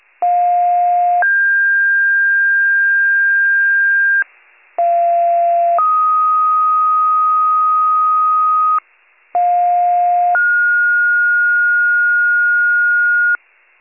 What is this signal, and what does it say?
Two-tone page, heard as a one-second tone, then a three-second tone, about 710 then 1600 Hz, about 690 then 1250 Hz, about 710 then 1430 Hz, with noise. It is heard from 0.2 to 13.4 s.